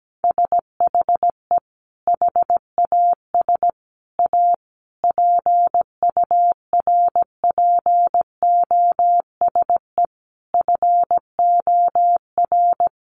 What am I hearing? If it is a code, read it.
Morse code, heard as a keyed tone, about 17 wpm, 700 Hz. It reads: SHE HAS A PURPOSE FOR
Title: Sentences from Top 800 Words 17wpm